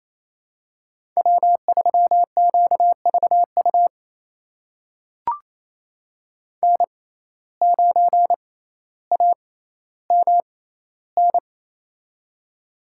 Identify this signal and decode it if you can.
Morse code, heard as a keyed tone, about 27 wpm, spaced as for 14 wpm, 700 Hz. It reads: W3QVU N9AMN
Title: USA Call Signs - 5 Characters - 6x Character Spacing - 28wpm